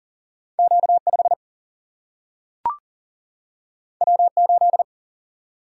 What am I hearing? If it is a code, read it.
Morse code, heard as a keyed tone, about 38 wpm, 700 Hz. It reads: Q5 W8
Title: Single-Letter Single-Number 40wpm